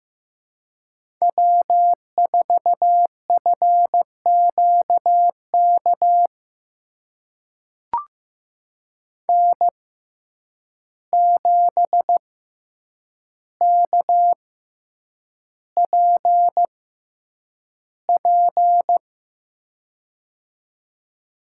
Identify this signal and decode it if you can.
Morse code, heard as a keyed tone, about 15 wpm, spaced as for 8 wpm, 700 Hz. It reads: W4FQK N7KPP
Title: USA Call Signs - 5 Characters - 6x Character Spacing - 15wpm